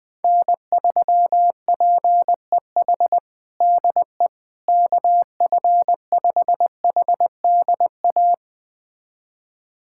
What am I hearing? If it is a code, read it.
Morse code, heard as a keyed tone, about 20 wpm, 700 Hz. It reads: N3PEH DE KF5HDA
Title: USA Callsign DE Callsign 20wpm